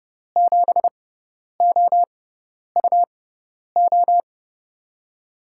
Morse code, heard as a keyed tone, about 30 wpm, spaced as for 10 wpm, 700 Hz. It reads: 7OUO
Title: USA Call Signs - 6 Characters - 6x Character Spacing - 30wpm